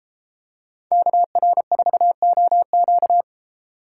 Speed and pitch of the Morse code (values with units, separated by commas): 31 wpm, 700 Hz